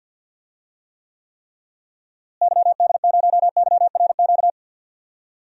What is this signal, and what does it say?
Morse code, heard as a keyed tone, about 47 wpm, 700 Hz. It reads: YD0YRQ
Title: DXCC-100 Call Signs - 6 Characters - Repeated 3x - 50wpm